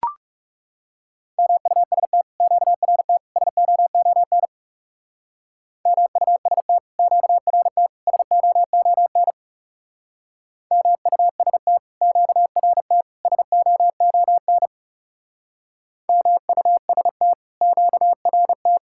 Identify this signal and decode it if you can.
Morse code, heard as a keyed tone, about 37 wpm, 700 Hz. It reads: MUST QRT SOON MUST QRT SOON MUST QRT SOON MUST QRT
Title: Speed-Racing Common QSO Phrases Encore - 30wpm